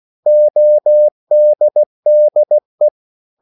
Morse code, heard as a keyed tone, about 16 wpm, 600 Hz. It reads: ODDE
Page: Korte danske ord 16wpm